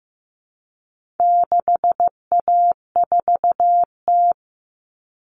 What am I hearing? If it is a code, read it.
Morse code, heard as a keyed tone, about 15 wpm, 700 Hz. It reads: 6A4T
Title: Common Cut Numbers - 4-Digit 15wpm